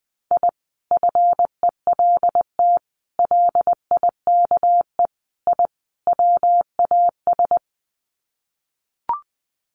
Morse code, heard as a keyed tone, about 20 wpm, 700 Hz. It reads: I FELT LIKE I WAS
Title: Sets of 5 Words Encore 20wpm